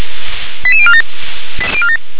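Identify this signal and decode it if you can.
Five-tone selective call (ZVEI), heard as quick runs of five tones, tones of about 70 ms. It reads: 88047
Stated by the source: ZVEI five-tone selective call received with an SDR.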